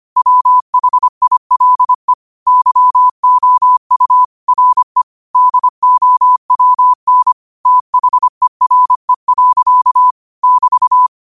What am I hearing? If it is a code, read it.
Morse code, heard as a keyed tone, about 25 wpm, 1000 Hz. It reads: WHILE YOU RE DOWN THERE. <BT>
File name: qotdfiles / 25wpm